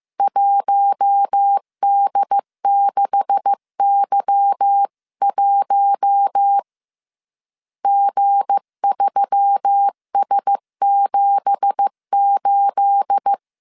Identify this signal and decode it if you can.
Morse code, heard as a keyed tone, about 15 wpm, 800 Hz. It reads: 1D6Y1 G3S78
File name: CW / DET / 1 9 1500C S C 14wpm 3245